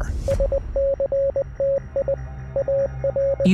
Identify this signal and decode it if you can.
Morse code, heard as a keyed tone, about 20 wpm, 550 Hz. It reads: SCTI AR
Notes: the plane's final transmission… a message in Morse code